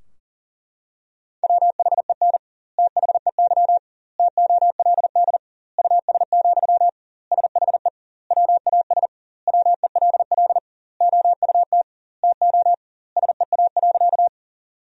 Morse code, heard as a keyed tone, about 40 wpm, 700 Hz. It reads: WHEN THEY TOLD US, SHE WAS WELL OUT TO SEA.
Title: 2x Word-Spacing - Sentences from Top 400 Words - 40wpm